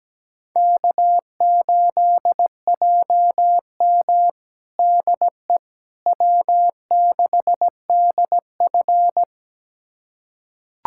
Morse code, heard as a keyed tone, about 17 wpm, 700 Hz. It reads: K8JM DE W6DF
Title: USA Callsign DE Callsign - 4 Characters - 17wpm